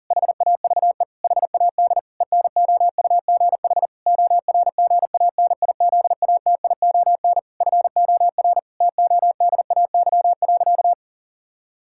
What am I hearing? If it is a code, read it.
Morse code, heard as a keyed tone, about 40 wpm, 700 Hz. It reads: HAVE HAD ENOUGH ORGANIZATION FOR TODAY.
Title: Sentences from Top 1000 Words - Review Part 5 - 40wpm